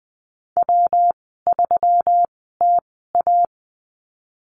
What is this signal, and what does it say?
Morse code, heard as a keyed tone, about 20 wpm, spaced as for 15 wpm, 700 Hz. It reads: W3TA
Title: 173 Lesson 28g - Number 9 - Call Signs 20wpm